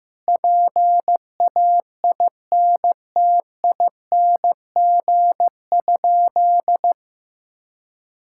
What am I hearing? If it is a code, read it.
Morse code, heard as a keyed tone, about 15 wpm, 700 Hz. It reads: PAINTING?
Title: Sentences from Top 1000 Words - Review Part 6 - 15wpm